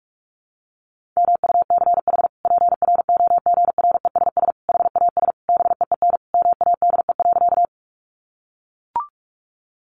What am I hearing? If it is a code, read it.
Morse code, heard as a keyed tone, about 44 wpm, 700 Hz. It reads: MUCH PROGRESS HAS BEEN MADE.
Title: Sentences from Top 1400 Words 45wpm